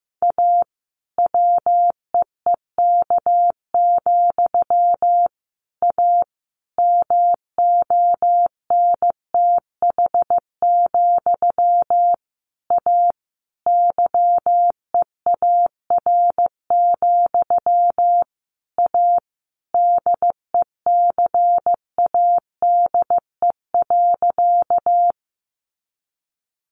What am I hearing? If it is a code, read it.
Morse code, heard as a keyed tone, about 15 wpm, 700 Hz. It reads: A WEEK, A MONTH, A YEAR, A DECADE.
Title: A Sound Of Thunder 15wpm